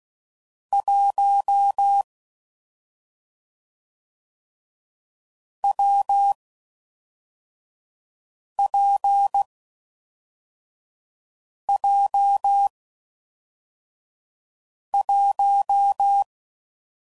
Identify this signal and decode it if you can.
Morse code, heard as a keyed tone, about 16 wpm, 800 Hz. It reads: 1 W P J 1
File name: Combinations / Difficult Combos / WPJ WPJ1